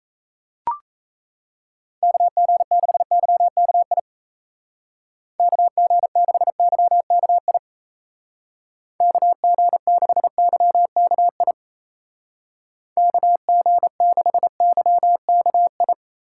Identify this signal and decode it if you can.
Morse code, heard as a keyed tone, about 35 wpm, 700 Hz. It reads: KG6YKI KG6YKI KG6YKI KG6YKI
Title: Speed Racing - USA 2x3 Call Signs 28wpm